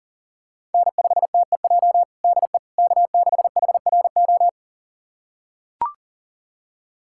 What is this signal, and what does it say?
Morse code, heard as a keyed tone, about 40 wpm, 700 Hz. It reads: N5TEJ DE K6HRO
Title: USA Callsign DE Callsign 40wpm